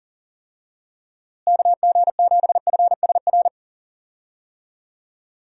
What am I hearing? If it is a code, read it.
Morse code, heard as a keyed tone, about 40 wpm, 700 Hz. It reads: KG7FSR
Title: USA Call Signs - 6 Characters - Repeated 2x - 40wpm